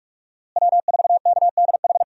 Morse code, heard as a keyed tone, about 42 wpm, 700 Hz. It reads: W4KDH
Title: USA Call Signs - 5 Characters - Repeated 2x - 45wpm